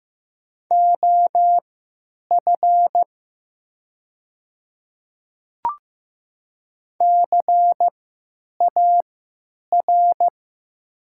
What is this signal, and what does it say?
Morse code, heard as a keyed tone, about 15 wpm, spaced as for 9 wpm, 700 Hz. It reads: OF CAR
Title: 3x Character Spacing - Top 300 Words - 15wpm